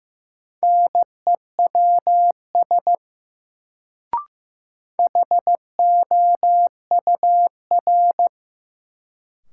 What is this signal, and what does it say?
Morse code, heard as a keyed tone, about 15 wpm, 700 Hz. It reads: NEWS HOUR
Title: Sets of 5 Words - New Word Drill - 15wpm